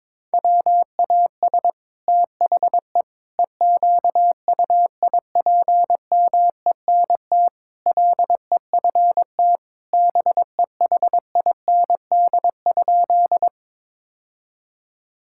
Morse code, heard as a keyed tone, about 22 wpm, 700 Hz. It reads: WAS THE EQUIPMENT LEFT BEHIND?
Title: Sentences from Top 1000 Words - Review Part 5 - 22wpm